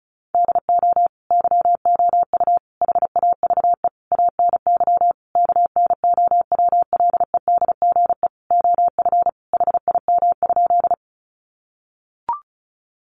Morse code, heard as a keyed tone, about 34 wpm, 700 Hz. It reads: DO YOU HAVE ANY KNOWLEDGE OF HIM?
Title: Sentences from Top 600 Words 35wpm